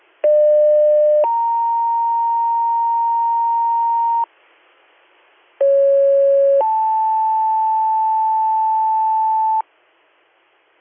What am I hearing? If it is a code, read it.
Two-tone page, heard as a one-second tone, then a three-second tone, about 600 then 930 Hz, about 550 then 880 Hz. It is heard from 0.2 to 9.6 s.